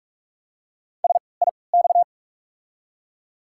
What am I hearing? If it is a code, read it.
Morse code, heard as a keyed tone, about 42 wpm, spaced as for 25 wpm, 690 Hz. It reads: SIX
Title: 3x Character Spacing - Top 500 Words - 45wpm